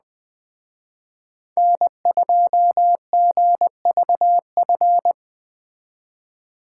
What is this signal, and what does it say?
Morse code, heard as a keyed tone, about 20 wpm, 700 Hz. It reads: N2GVF
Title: Speed Racing - USA 1x3 Call Signs 17wpm